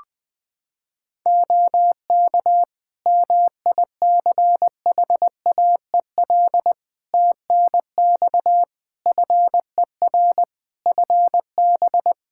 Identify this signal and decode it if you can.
Morse code, heard as a keyed tone, about 20 wpm, 700 Hz. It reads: OK MICHAEL TNX FER FB
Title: Practice QSO - Exchange 4 - Ending - 20wpm